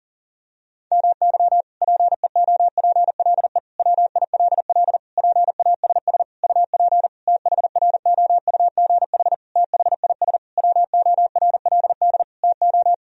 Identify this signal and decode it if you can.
Morse code, heard as a keyed tone, about 37 wpm, 700 Hz. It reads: MY PEOPLE WILL PASS UP THROUGH THIS WORLD TO
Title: Sentences from Top 1000 Words - Review Part 2 - 40wpm